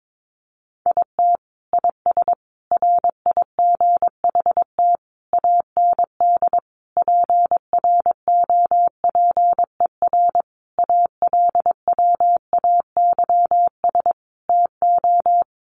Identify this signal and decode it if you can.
Morse code, heard as a keyed tone, about 22 wpm, 700 Hz. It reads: IT IS RIGHT AND PROPER ALWAYS TO
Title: Sentences from Top 1500 Words 22wpm